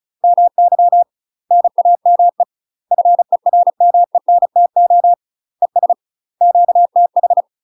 Morse code, heard as a keyed tone, about 34 wpm, 700 Hz. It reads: MY NAME FERMENTO ES QTH
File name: UK MASTER QSOs TRAINING / 35WPM / UK MASTER QSOs 0037